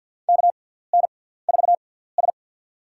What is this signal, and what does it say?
Morse code, heard as a keyed tone, about 51 wpm, spaced as for 17 wpm, 700 Hz. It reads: KN4S
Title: USA Call Signs - 4 Characters - 6x Character Spacing - 50wpm